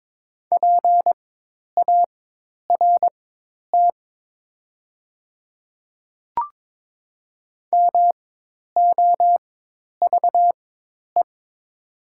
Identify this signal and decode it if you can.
Morse code, heard as a keyed tone, about 22 wpm, spaced as for 10 wpm, 700 Hz. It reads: PART MOVE
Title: 4x Character Spacing - Top 200 Words - 22wpm